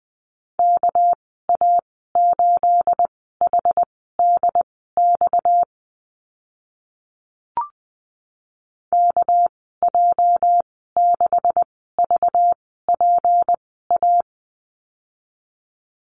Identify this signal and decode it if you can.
Morse code, heard as a keyed tone, about 20 wpm, spaced as for 14 wpm, 700 Hz. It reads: KA8HDX KJ6VPA
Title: 238 Lesson 38g - Letter q - Call Signs 20wpm